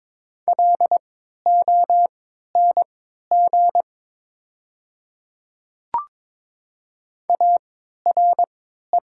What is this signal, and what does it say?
Morse code, heard as a keyed tone, about 22 wpm, spaced as for 12 wpm, 700 Hz. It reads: LONG ARE
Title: 3x Character Spacing - Top 200 Words - 22wpm